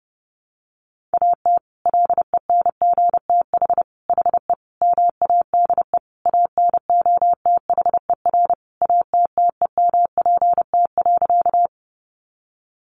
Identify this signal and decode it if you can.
Morse code, heard as a keyed tone, about 30 wpm, 700 Hz. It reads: AT LENGTH HE MADE ANOTHER ATTEMPT.
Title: Sentences from Top 1000 Words 30wpm